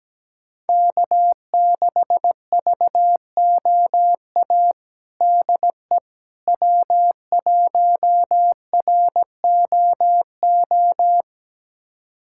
Morse code, heard as a keyed tone, about 17 wpm, 700 Hz. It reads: K6VOA DE W1ROO
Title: USA Callsign DE Callsign - 5 Characters - 17wpm